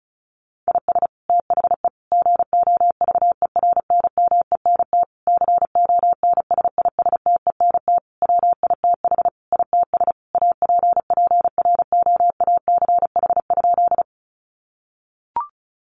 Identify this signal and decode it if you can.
Morse code, heard as a keyed tone, about 34 wpm, 700 Hz. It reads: IS THE GOVERNMENT CONSISTENT WITH ITS APPROACH?
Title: Sentences from Top 1500 Words 35wpm